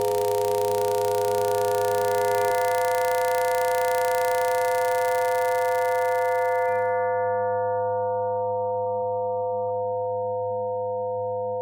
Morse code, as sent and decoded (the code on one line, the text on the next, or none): none